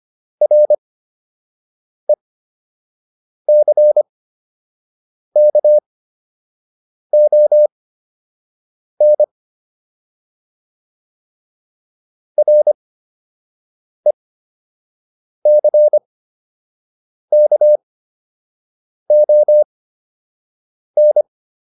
Morse code, heard as a keyed tone, about 25 wpm, spaced as for 6 wpm, 600 Hz. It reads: RECKON RECKON